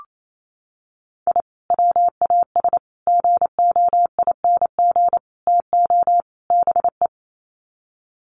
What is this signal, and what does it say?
Morse code, heard as a keyed tone, about 28 wpm, 700 Hz. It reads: I WAS GOING TO BE
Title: Sets of 5 Words 28wpm